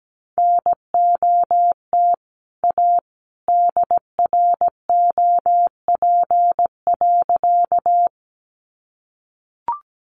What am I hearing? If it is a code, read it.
Morse code, heard as a keyed tone, about 17 wpm, 700 Hz. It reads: NOT A DROP.
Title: Sentences from Top 1400 Words 17wpm